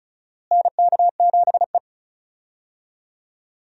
Morse code, heard as a keyed tone, about 36 wpm, 700 Hz. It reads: NK7E
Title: USA Call Signs - 4 Characters - Repeated 2x - 35wpm